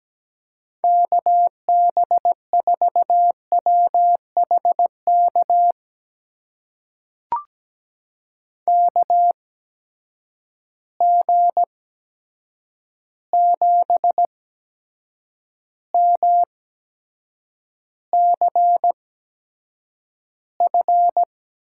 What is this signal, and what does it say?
Morse code, heard as a keyed tone, about 17 wpm, spaced as for 7 wpm, 700 Hz. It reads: KB4WHK KG7MCF
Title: USA Call Signs - 6 Characters - 8x Character Spacing - 17wpm